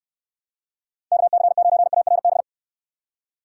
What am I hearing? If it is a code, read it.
Morse code, heard as a keyed tone, about 69 wpm, 700 Hz. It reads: KC9ARB